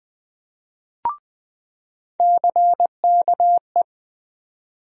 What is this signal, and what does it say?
Morse code, heard as a keyed tone, about 20 wpm, 700 Hz. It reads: CKE
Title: USA Call Sign Postfixes - 3 Letters 20wpm